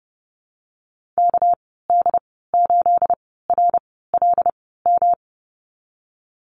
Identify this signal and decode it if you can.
Morse code, heard as a keyed tone, about 30 wpm, spaced as for 17 wpm, 700 Hz. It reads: KD8RLM
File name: Lesson 37g - Letter x - Call Signs 30wpm